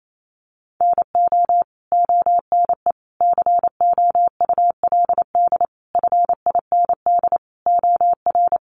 Morse code, heard as a keyed tone, about 28 wpm, 700 Hz. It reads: NO ONE COULD FIND OR
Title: Sentences from Top 1100 Words 28wpm